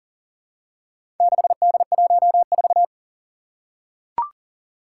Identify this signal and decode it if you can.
Morse code, heard as a keyed tone, about 39 wpm, 700 Hz. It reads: 6D14
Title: Four Challenging Letter-Number Combinations 40wpm